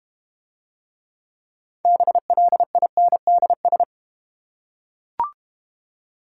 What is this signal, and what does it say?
Morse code, heard as a keyed tone, about 32 wpm, 700 Hz. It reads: BLINDS